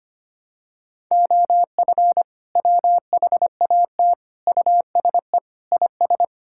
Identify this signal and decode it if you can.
Morse code, heard as a keyed tone, about 25 wpm, 700 Hz. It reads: OF WHAT USE IS
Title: Sentences from Top 1000 Words - Review Part 5 - 25wpm